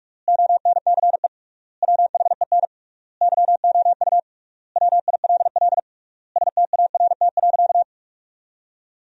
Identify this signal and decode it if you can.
Morse code, heard as a keyed tone, about 43 wpm, 700 Hz. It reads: ONCE WHEN YOU WILL START.
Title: 3x Word-Spacing - Sentences from Top 300 Words - 45wpm